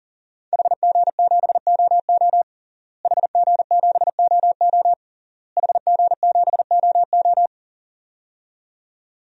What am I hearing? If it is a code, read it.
Morse code, heard as a keyed tone, about 40 wpm, 700 Hz. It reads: HG7OO HG7OO HG7OO
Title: DXCC-100 Call Signs - 5 Characters - Repeated 3x - 40wpm